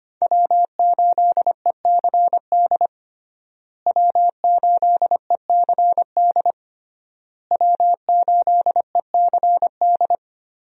Morse code, heard as a keyed tone, about 25 wpm, 700 Hz. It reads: W8ECD W8ECD W8ECD
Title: USA Call Signs - 5 Characters - Repeated 3x - 25wpm